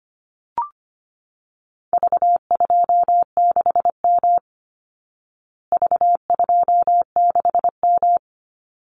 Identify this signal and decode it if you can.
Morse code, heard as a keyed tone, about 25 wpm, 700 Hz. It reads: V26M V26M
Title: Call Signs Repeated 2x - 25wpm